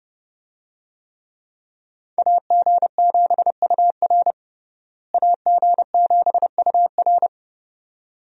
Morse code, heard as a keyed tone, about 30 wpm, 700 Hz. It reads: AG7UR AG7UR
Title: USA Call Signs - 5 Characters - Repeated 2x - 30wpm